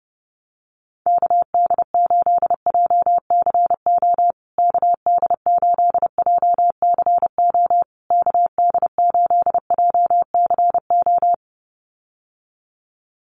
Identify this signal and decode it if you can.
Morse code, heard as a keyed tone, about 30 wpm, 700 Hz. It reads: KD8JCO KD8JCO KD8JCO
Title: USA Call Signs Repeated 3x 30wpm